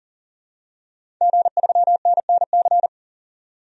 Morse code, heard as a keyed tone, about 37 wpm, 700 Hz. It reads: G3NNC
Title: DXCC-100 Call Signs - 5 Characters - 40wpm